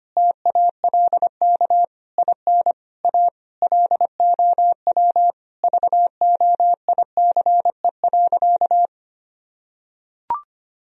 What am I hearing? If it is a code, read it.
Morse code, heard as a keyed tone, about 25 wpm, 700 Hz. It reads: TALK IN A LOW VOICE.
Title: Sentences from Top 1000 Words - Review Part 1 - 25wpm